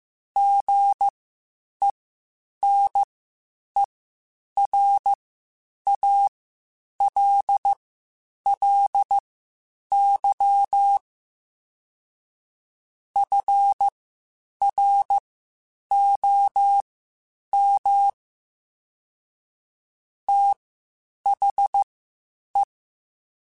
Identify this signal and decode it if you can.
Morse code, heard as a keyed tone, about 15 wpm, spaced as for 8 wpm, 800 Hz. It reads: GENERALLY FROM THE
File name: Spaced Characters / Spaced Characters-mp3s / Weather 4 21 sp @13